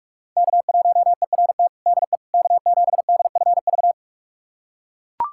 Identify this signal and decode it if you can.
Morse code, heard as a keyed tone, about 41 wpm, 700 Hz. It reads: K1ERT DE K7DFV
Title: USA Callsign DE Callsign - 5 Characters - 45wpm